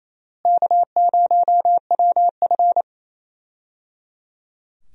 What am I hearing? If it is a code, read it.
Morse code, heard as a keyed tone, about 28 wpm, 700 Hz. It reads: K0WF
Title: USA Call Signs - 4 Characters - Repeated 3x - 28wpm